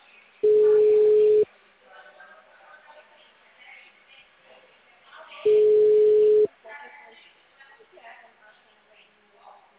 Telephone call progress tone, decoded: ringback tone

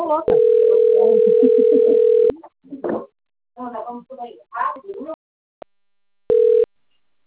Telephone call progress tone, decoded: ringback tone

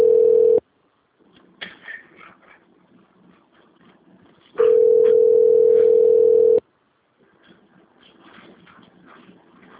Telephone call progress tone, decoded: ringback tone

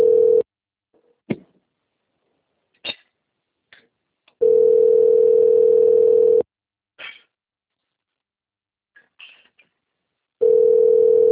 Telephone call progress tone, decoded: ringback tone

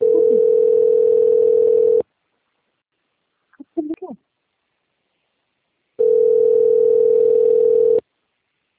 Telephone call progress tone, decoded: ringback tone

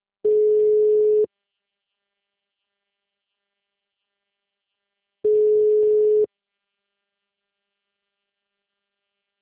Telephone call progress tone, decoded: ringback tone

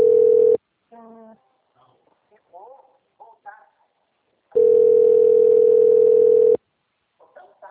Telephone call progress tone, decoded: ringback tone

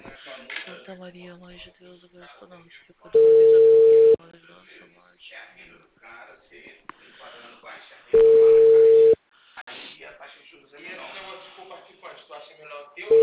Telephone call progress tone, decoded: ringback tone

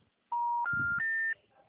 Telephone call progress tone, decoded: special information tone

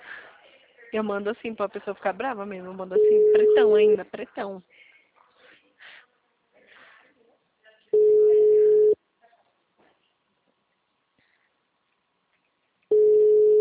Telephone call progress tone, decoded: ringback tone

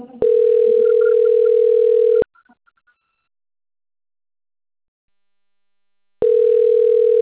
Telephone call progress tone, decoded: ringback tone

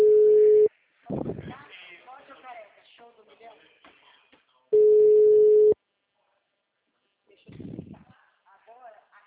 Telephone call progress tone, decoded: ringback tone

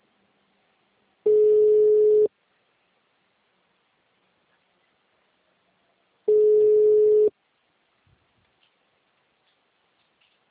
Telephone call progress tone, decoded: ringback tone

